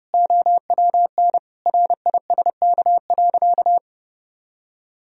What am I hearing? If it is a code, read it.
Morse code, heard as a keyed tone, about 30 wpm, 700 Hz. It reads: OWN RISK.
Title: Sentences from Top 1000 Words - Review Part 4 - 30wpm